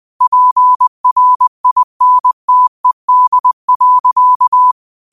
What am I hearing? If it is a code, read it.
Morse code, heard as a keyed tone, about 20 wpm, 1000 Hz. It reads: PRINTED.
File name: qotdfiles / 20wpm